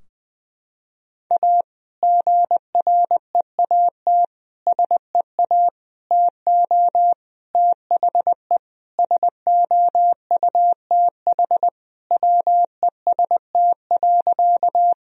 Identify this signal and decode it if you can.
Morse code, heard as a keyed tone, about 20 wpm, 700 Hz. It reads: A GREAT SEA TO THE SOUTH WEST.
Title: Sentences from Top 1000 Words - Review Part 2 - 20wpm